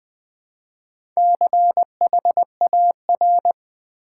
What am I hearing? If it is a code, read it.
Morse code, heard as a keyed tone, about 20 wpm, 700 Hz. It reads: CHAR
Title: Build A Word - 4 Letters At A Time - 12-Letter Words - 20wpm